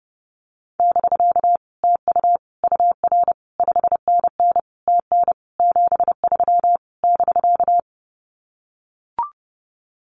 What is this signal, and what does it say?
Morse code, heard as a keyed tone, about 32 wpm, 700 Hz. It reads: <BK> TU UR 5NN TN 73 <BK>
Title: Common QSO Phrases Encore 30wpm